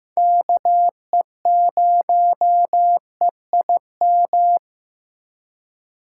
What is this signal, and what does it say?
Morse code, heard as a keyed tone, about 15 wpm, 700 Hz. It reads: KE0EIM
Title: USA Call Signs - 4x Character Spacing - 15wpm